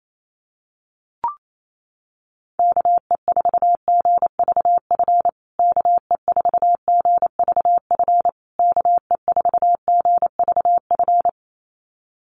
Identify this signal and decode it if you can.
Morse code, heard as a keyed tone, about 28 wpm, 700 Hz. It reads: KE4GVF KE4GVF KE4GVF
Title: USA Call Signs Repeated 3x 28wpm